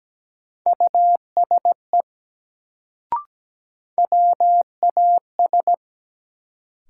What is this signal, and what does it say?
Morse code, heard as a keyed tone, about 17 wpm, 700 Hz. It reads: USE WAS
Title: Sets of 2 Words - New Word Drill - 17wpm